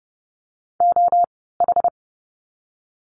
Morse code, heard as a keyed tone, about 30 wpm, spaced as for 17 wpm, 700 Hz. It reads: OH
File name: Lesson 16f - Letter c - Words 30wpm